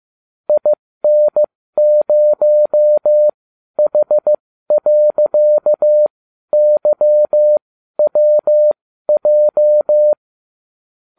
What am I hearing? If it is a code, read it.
Morse code, heard as a keyed tone, about 15 wpm, spaced as for 12 wpm, 600 Hz. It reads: IN0H.YWJ